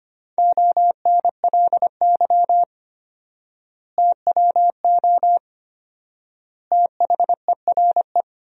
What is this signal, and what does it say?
Morse code, heard as a keyed tone, about 25 wpm, 700 Hz. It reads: ONLY TWO THERE
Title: 4x Word-Spacing - Sentences from Top 200 Words - 25wpm